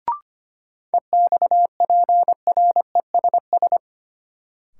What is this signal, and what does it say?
Morse code, heard as a keyed tone, about 25 wpm, 700 Hz. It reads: EXPRESS
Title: Sentences from Top 1100 Words - New Word Drill Alt - 25wpm